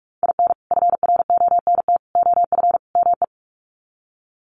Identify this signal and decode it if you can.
Morse code, heard as a keyed tone, about 44 wpm, 700 Hz. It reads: IN FRONT OF ME
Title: Sets of 4 Words 45wpm